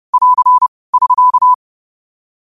Morse code, heard as a keyed tone, about 30 wpm, 1000 Hz. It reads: <AR> <SK>
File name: qotdfiles / 30wpm